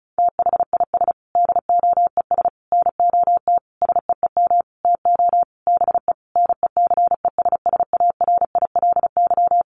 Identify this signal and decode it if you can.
Morse code, heard as a keyed tone, about 34 wpm, 700 Hz. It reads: THIS DOES NOT SEEM TO BE NECESSARILY
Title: Sentences from Top 1400 Words 35wpm